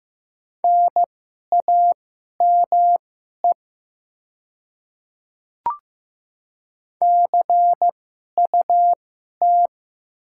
Morse code, heard as a keyed tone, about 15 wpm, spaced as for 11 wpm, 700 Hz. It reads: NAME CUT
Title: 2x Character Spacing - Top 300 Words - 15wpm